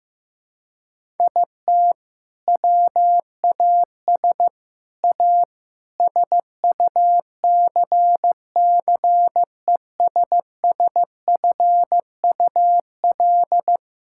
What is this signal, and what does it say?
Morse code, heard as a keyed tone, about 15 wpm, 700 Hz. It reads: IT WAS A SUCCESSFUL
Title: Sentences from Top 1000 Words - Review Part 5 - 15wpm